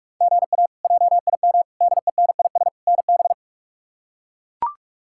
Morse code, heard as a keyed tone, about 43 wpm, 700 Hz. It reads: GA JIM DENIS NB
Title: K1USN SST QSO Practice - 45wpm